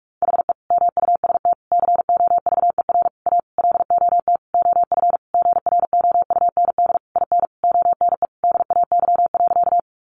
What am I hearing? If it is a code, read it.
Morse code, heard as a keyed tone, about 45 wpm, 700 Hz. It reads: HE MUST COVER A LOT OF GROUND IN ONE DAY.
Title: Sentences from Top 500 Words 45wpm